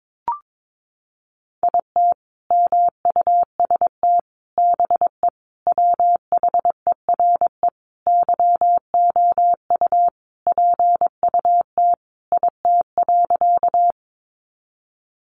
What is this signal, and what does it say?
Morse code, heard as a keyed tone, about 22 wpm, 700 Hz. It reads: IT MUST BE WHERE YOU PUT IT.
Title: Sentences from Top 200 Words 22wpm